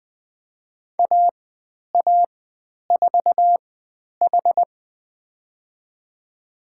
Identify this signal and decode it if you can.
Morse code, heard as a keyed tone, about 21 wpm, spaced as for 10 wpm, 700 Hz. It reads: AA4H
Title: 109 Lesson 18g - Letter m - Call Signs 20wpm